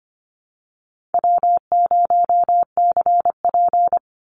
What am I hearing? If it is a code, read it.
Morse code, heard as a keyed tone, about 25 wpm, 700 Hz. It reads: W0CP
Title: NA SOTA Activator Callsigns 25wpm